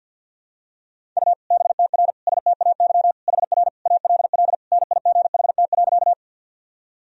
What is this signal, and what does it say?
Morse code, heard as a keyed tone, about 52 wpm, 700 Hz. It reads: U BTR STAY HR ALL NIGHT.
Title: Sentences from Top 500 Words and Abbreviations 50wpm